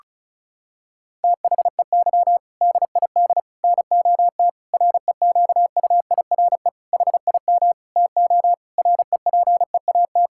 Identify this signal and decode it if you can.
Morse code, heard as a keyed tone, about 36 wpm, 700 Hz. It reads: THEY DID NOT REQUIRE HIM TO REPEAT
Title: Sentences from Top 1600 Words 35wpm